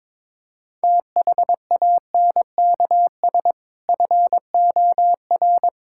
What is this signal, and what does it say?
Morse code, heard as a keyed tone, about 22 wpm, 700 Hz. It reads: THANKS FOR
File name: practice / 22wpm / Sentences with 2-Word Phrases - 22wpm